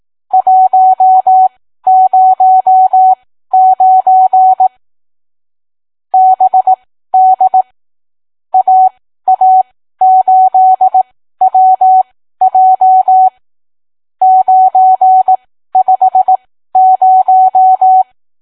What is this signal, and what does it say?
Morse code, heard as a keyed tone, about 18 wpm, spaced as for 13 wpm, 750 Hz. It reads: 109 BD AA8WJ 950